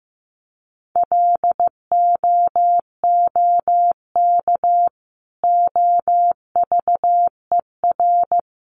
Morse code, heard as a keyed tone, about 15 wpm, 700 Hz. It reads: LOOK OVER
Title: Sentences from Top 100 Words 15wpm